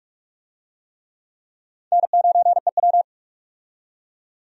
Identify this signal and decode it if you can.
Morse code, heard as a keyed tone, about 46 wpm, 700 Hz. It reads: N9EW
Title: USA Call Signs - 4 Characters - 45wpm